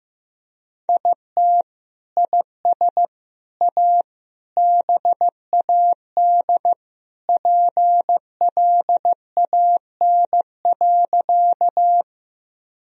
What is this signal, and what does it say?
Morse code, heard as a keyed tone, about 15 wpm, 700 Hz. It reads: IT IS A BAD PLAN.
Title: Sentences from Top 1000 Words - Review Part 2 - 15wpm